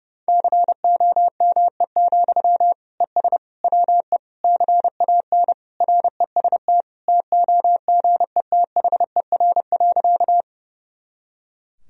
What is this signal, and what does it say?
Morse code, heard as a keyed tone, about 32 wpm, 700 Hz. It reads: COME, ES WE CAN REST TOGETHER.
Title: Sentences from Top 500 Words and Abbreviations 30wpm